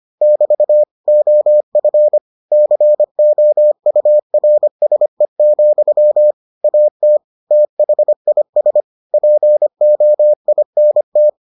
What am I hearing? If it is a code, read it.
Morse code, heard as a keyed tone, about 25 wpm, 600 Hz. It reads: <BT> OF COURSE, AT THIS POINT